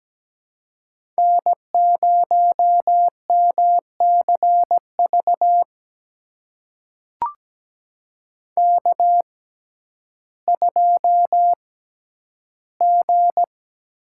Morse code, heard as a keyed tone, about 17 wpm, spaced as for 10 wpm, 700 Hz. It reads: N0MCV K2G
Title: USA Call Signs - 5 Characters - 6x Character Spacing - 17wpm